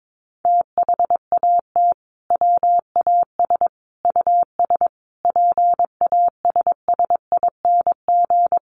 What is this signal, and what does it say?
Morse code, heard as a keyed tone, about 22 wpm, 700 Hz. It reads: THAT WAS US PASSING
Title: A Sound Of Thunder 22wpm